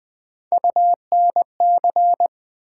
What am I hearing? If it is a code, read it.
Morse code, heard as a keyed tone, about 20 wpm, 700 Hz. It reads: UNC
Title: Build A Word - 1 Letter At A Time - 5-Letter Words - 20wpm